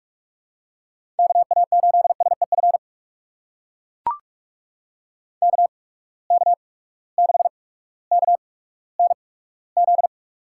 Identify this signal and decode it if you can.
Morse code, heard as a keyed tone, about 43 wpm, spaced as for 19 wpm, 700 Hz. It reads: KA8SEF KK6KNZ
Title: USA Call Signs - 6 Characters - 8x Character Spacing - 45wpm